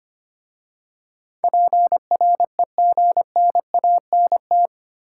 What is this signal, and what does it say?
Morse code, heard as a keyed tone, about 25 wpm, 700 Hz. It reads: PREGNANT
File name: 8 Letter Words - First 3 Letters Given - 25wpm